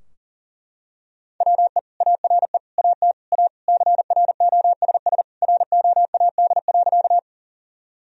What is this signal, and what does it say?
Morse code, heard as a keyed tone, about 40 wpm, 700 Hz. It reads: WE ARE AT A CROSS ROAD.
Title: Sentences from Top 1000 Words - Review Part 2 - 40wpm